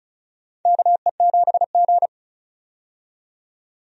Morse code, heard as a keyed tone, about 36 wpm, 700 Hz. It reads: KE7G
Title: USA Call Signs - 4 Characters - 35wpm